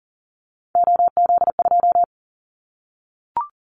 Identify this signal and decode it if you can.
Morse code, heard as a keyed tone, about 40 wpm, 700 Hz. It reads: OZ2
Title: Callsign Prefixes for Top 50 DXCC - 40wpm